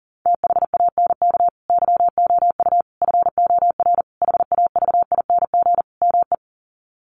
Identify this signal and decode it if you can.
Morse code, heard as a keyed tone, about 40 wpm, 700 Hz. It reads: THANK YOU FOR HAVING ME
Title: Sets of 5 Words Encore 40wpm